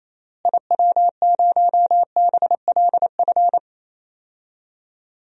Morse code, heard as a keyed tone, about 27 wpm, 700 Hz. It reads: IW0BLF
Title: DXCC-100 Call Signs - 6 Characters - Repeated 3x - 28wpm